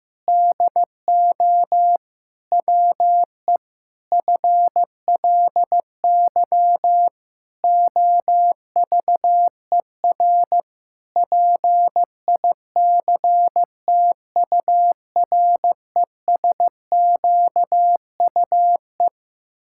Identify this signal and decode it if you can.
Morse code, heard as a keyed tone, about 15 wpm, 700 Hz. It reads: DO WE FLY OVER PICTURESQUE
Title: Pangram Challenge v2 - 15wpm